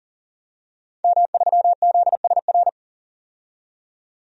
Morse code, heard as a keyed tone, about 37 wpm, 700 Hz. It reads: M3ZSR
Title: DXCC-100 Call Signs - 5 Characters - 40wpm